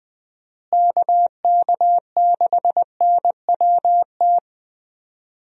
Morse code, heard as a keyed tone, about 20 wpm, 700 Hz. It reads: KK6NWT
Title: USA Call Signs - 6 Characters - Repeated 2x - 20wpm